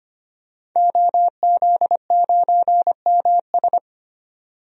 Morse code, heard as a keyed tone, about 25 wpm, 700 Hz. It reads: OZ9MS
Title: DXCC-100 Call Signs - 5 Characters - 25wpm